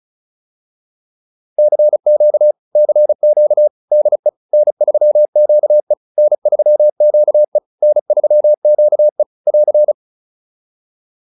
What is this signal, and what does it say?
Morse code, heard as a keyed tone, about 34 wpm, 600 Hz. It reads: CQ CQ DE N3QE N3QE N3QE <AR>